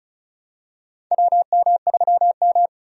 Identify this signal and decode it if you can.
Morse code, heard as a keyed tone, about 36 wpm, 700 Hz. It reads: WM3M
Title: USA Call Signs - 4 Characters - 35wpm